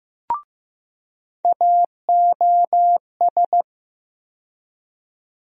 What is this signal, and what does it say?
Morse code, heard as a keyed tone, about 15 wpm, 700 Hz. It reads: AOS
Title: USA Call Sign Postfixes - 3 Letters 15wpm